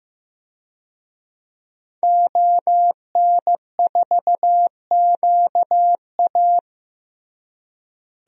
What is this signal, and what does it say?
Morse code, heard as a keyed tone, about 15 wpm, 700 Hz. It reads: ON4QA
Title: DXCC-100 Call Signs - 5 Characters - Repeated 2x - 15wpm